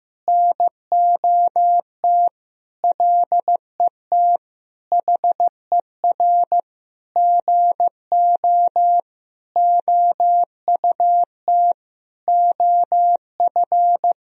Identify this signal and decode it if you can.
Morse code, heard as a keyed tone, about 16 wpm, 700 Hz. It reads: NOT LET HER GO OUT OF
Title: Sentences from Top 1000 Words - Review Part 2 - 15wpm